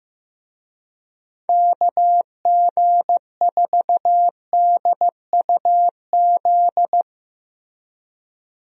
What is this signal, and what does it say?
Morse code, heard as a keyed tone, about 15 wpm, 700 Hz. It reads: KG4DUZ
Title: USA Call Signs - 6 Characters - Repeated 3x - 15wpm